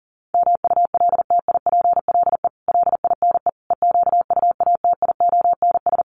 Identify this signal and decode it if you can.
Morse code, heard as a keyed tone, about 40 wpm, 700 Hz. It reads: MULTIPLE LINE EQUATIONS
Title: Sets of 5 Words 40wpm